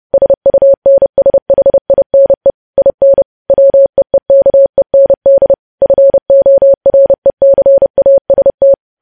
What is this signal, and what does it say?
Morse code, heard as a keyed tone, about 30 wpm, 550 Hz. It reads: SUNSHINE IN WEEKEND FORECAST
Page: news headlines in morse code at 30 wpm